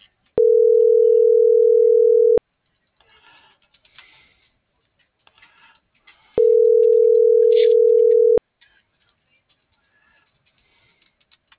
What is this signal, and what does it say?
Ringback tone (North America), 440 + 480 Hz.